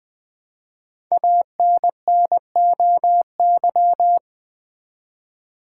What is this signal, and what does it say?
Morse code, heard as a keyed tone, about 20 wpm, 700 Hz. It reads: ANNOY
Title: Build A Word - 1 Letter At A Time - 5-Letter Words - 20wpm